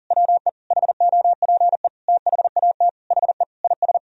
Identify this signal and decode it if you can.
Morse code, heard as a keyed tone, about 37 wpm, 700 Hz. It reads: WE HOPE THAT HE IS
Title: Sentences from Top 1000 Words - Review Part 1 - 40wpm